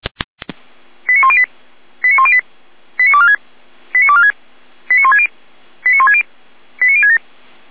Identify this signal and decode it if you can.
Five-tone selective call (ZVEI), heard as quick runs of five tones, tones of about 70 ms. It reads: 89108 89108 89246 89246 89160 89160 89067
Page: /455/SDR/ZVEI/89078/